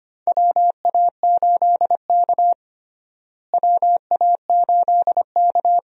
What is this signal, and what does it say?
Morse code, heard as a keyed tone, about 25 wpm, 700 Hz. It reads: WA8K WA8K
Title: USA Call Signs - 4 Characters - Repeated 2x - 25wpm